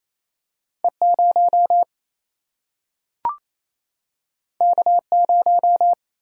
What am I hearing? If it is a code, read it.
Morse code, heard as a keyed tone, about 28 wpm, 700 Hz. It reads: E 0 K 0
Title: Single-Letter Single-Number 28wpm